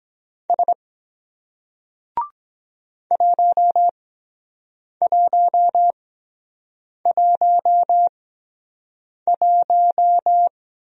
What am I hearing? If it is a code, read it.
Morse code, heard as a keyed tone, about 23 wpm, spaced as for 7 wpm, 700 Hz. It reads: S 1111